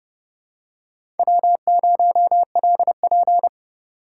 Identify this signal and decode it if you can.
Morse code, heard as a keyed tone, about 30 wpm, 700 Hz. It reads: W0LP
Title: USA Call Signs - 4 Characters - 4x Character Spacing - 30wpm